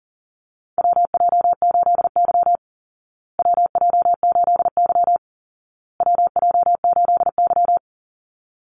Morse code, heard as a keyed tone, about 40 wpm, 700 Hz. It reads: WJ8Y WJ8Y WJ8Y
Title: Call Signs Repeated 2x - 40wpm